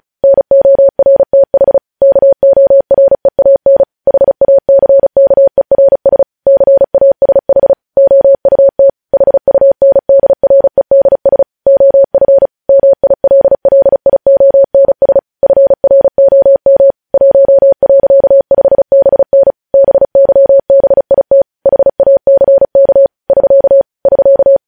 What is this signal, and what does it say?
Morse code, heard as a keyed tone, about 36 wpm, 550 Hz. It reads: NORTH KOREAN HACKERS CASH OUT HUNDREDS OF MILLIONS FROM 1.5BN BYBIT HACK <SK> <SK>
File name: podcast / News headlines-1x-35x35